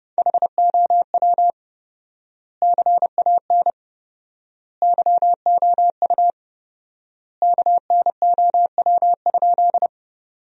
Morse code, heard as a keyed tone, about 32 wpm, 700 Hz. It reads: HOW CAN YOU KNOW?
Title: 4x Word-Spacing - Sentences from Top 100 Words - 30wpm